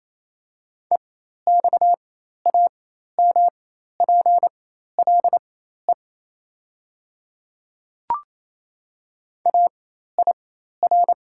Morse code, heard as a keyed tone, about 28 wpm, spaced as for 13 wpm, 700 Hz. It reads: EXAMPLE AIR
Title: 4x Character Spacing - Top 500 Words - 28wpm